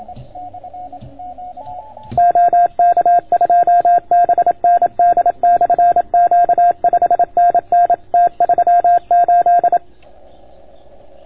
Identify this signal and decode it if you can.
Morse code, heard as a keyed tone, about 27 wpm, 690 Hz. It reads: OK2BND/Q 5NN T38